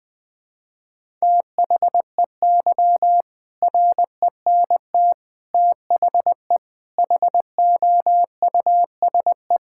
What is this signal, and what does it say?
Morse code, heard as a keyed tone, about 20 wpm, 700 Hz. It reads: THEY RENT THE HOUSE
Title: Sentences from Top 1000 Words - Review Part 5 - 20wpm